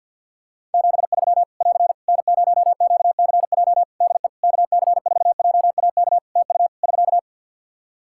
Morse code, heard as a keyed tone, about 49 wpm, 700 Hz. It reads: 73 <AR> N0QCJ DE KC4JAK TU <SK>
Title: K1USN SST QSO Practice 50wpm